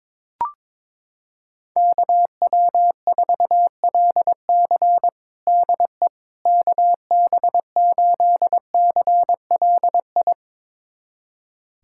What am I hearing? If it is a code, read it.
Morse code, heard as a keyed tone, about 22 wpm, 700 Hz. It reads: KW4LC DE KB8CLI
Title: USA Callsign DE Callsign 22wpm